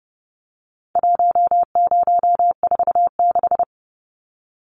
Morse code, heard as a keyed tone, about 30 wpm, 700 Hz. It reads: 1046
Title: Numbers - 4-Digit 30wpm